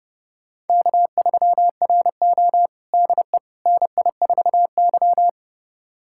Morse code, heard as a keyed tone, about 32 wpm, 700 Hz. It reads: K3RO DE NI4Y
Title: USA Callsign DE Callsign - 4 Characters - 30wpm